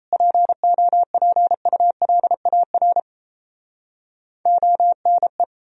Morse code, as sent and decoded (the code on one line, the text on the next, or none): .--. --- .--. ..- .-.. .- .-. / --- -. .
POPULAR ONE